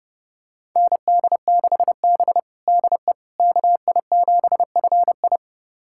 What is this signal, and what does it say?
Morse code, heard as a keyed tone, about 32 wpm, 700 Hz. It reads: ND6B DE KI7FI
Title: USA Callsign DE Callsign 30wpm